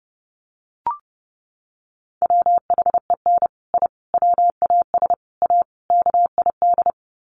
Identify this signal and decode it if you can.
Morse code, heard as a keyed tone, about 30 wpm, 700 Hz. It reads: WHEN I WAS A KID
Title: Sets of 5 Words 30wpm